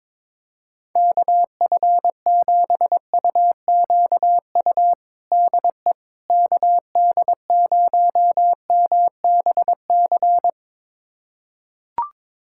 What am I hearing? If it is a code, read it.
Morse code, heard as a keyed tone, about 22 wpm, 700 Hz. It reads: KF7UQU DE KD0MBC
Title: USA Callsign DE Callsign - 6 Characters - 22wpm